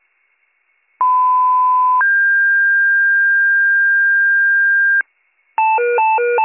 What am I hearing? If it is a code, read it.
Two-tone page, heard as a one-second tone, then a three-second tone, about 1010 then 1600 Hz. It is heard from 1.0 to 5.0 s.